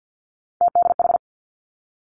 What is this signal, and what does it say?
Morse code, heard as a keyed tone, about 50 wpm, 700 Hz. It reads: TDH
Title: Three Challenging Letter-Number Combinations 50wpm